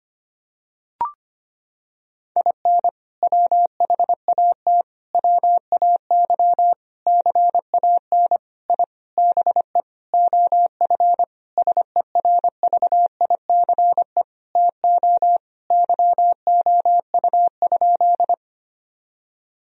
Morse code, heard as a keyed tone, about 25 wpm, 700 Hz. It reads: IN WHAT WAY CAN I BE OF SERVICE TO YOU?
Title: Sentences from Top 1000 Words - Review Part 4 - 25wpm